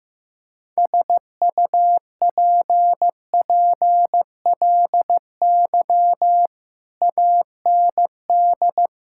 Morse code, heard as a keyed tone, about 15 wpm, 700 Hz. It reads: SUPPLY AND
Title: Sentences from Top 1000 Words - Review Part 6 - 15wpm